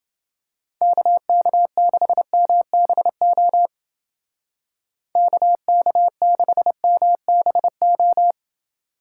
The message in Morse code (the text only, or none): KK6MBO KK6MBO